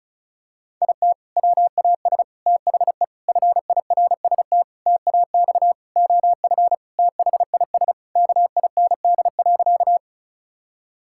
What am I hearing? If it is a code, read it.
Morse code, heard as a keyed tone, about 36 wpm, 700 Hz. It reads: IT WAS THE FIRST TAX OF THIS KIND.
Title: Sentences from Top 1000 Words - Review Part 5 - 35wpm